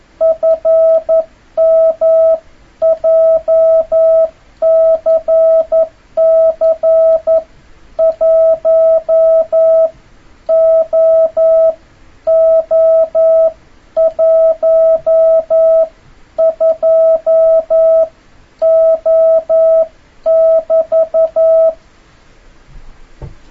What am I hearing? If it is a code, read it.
Morse code, heard as a keyed tone, about 11 wpm, spaced as for 9 wpm, 650 Hz. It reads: FM JCC 1OO12O <BT>
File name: jh8cbh / 2026cw